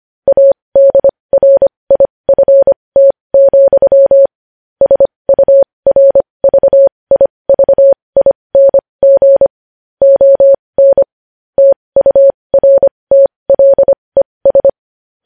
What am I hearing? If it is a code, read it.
Morse code, heard as a keyed tone, about 25 wpm, spaced as for 20 wpm, 550 Hz. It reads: ADRIFT, SURVIVING ON TURTLES
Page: News Headlines in Morse Code at 20 WPM